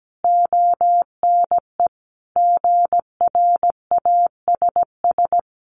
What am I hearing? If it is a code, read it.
Morse code, heard as a keyed tone, about 17 wpm, 700 Hz. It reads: ONE GRASS
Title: A Sound Of Thunder 17wpm